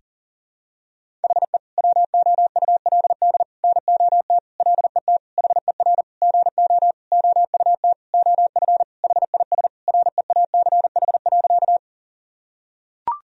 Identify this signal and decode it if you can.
Morse code, heard as a keyed tone, about 40 wpm, 700 Hz. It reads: HE WOULD NOT LET HER GO OUT OF HIS REACH.
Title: Sentences from Top 1000 Words - Review Part 2 - 40wpm